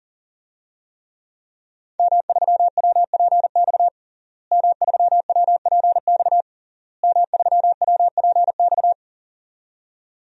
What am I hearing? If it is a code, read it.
Morse code, heard as a keyed tone, about 40 wpm, 700 Hz. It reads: M3WPX M3WPX M3WPX
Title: DXCC-100 Call Signs - 5 Characters - Repeated 3x - 40wpm